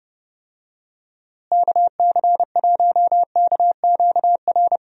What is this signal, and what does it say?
Morse code, heard as a keyed tone, about 32 wpm, 700 Hz. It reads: KC1KQR
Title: USA Call Signs - 6 Characters - Repeated 2x - 30wpm